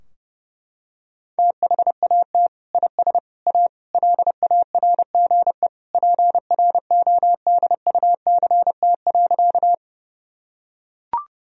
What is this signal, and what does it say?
Morse code, heard as a keyed tone, about 30 wpm, 700 Hz. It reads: THAT IS A LARGE PRODUCT.
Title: Sentences from Top 1000 Words - Review Part 2 - 30wpm